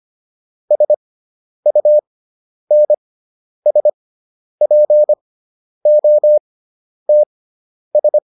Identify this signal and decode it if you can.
Morse code, heard as a keyed tone, about 25 wpm, spaced as for 10 wpm, 600 Hz. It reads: SUNSPOTS